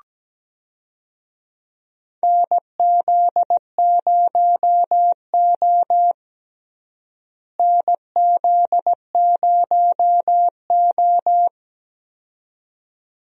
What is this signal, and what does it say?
Morse code, heard as a keyed tone, about 17 wpm, 700 Hz. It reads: NZ0O NZ0O
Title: USA Call Signs - 4 Characters - Repeated 2x - 17wpm